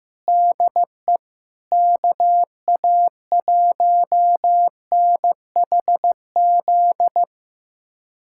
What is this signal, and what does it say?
Morse code, heard as a keyed tone, about 16 wpm, 700 Hz. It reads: DE KA1NHZ
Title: USA Callsign DE Callsign 15wpm